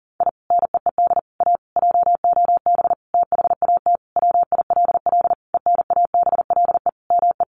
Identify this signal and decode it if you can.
Morse code, heard as a keyed tone, about 40 wpm, 700 Hz. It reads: I NEED A JOB THAT WILL ENABLE ME
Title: Sentences from Top 1500 Words 40wpm